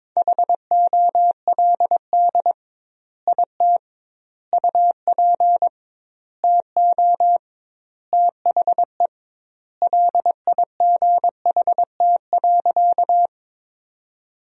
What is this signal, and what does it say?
Morse code, heard as a keyed tone, about 22 wpm, 700 Hz. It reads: HOLD IT UP TO THE LIGHT.
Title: 2x Word-Spacing - Sentences from Top 400 Words - 22wpm